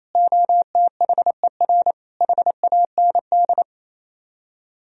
Morse code, heard as a keyed tone, about 35 wpm, 700 Hz. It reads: OTHER HAND